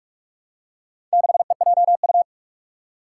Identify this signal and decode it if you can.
Morse code, heard as a keyed tone, about 44 wpm, 700 Hz. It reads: 6EJU
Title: Four Challenging Letter-Number Combinations 45wpm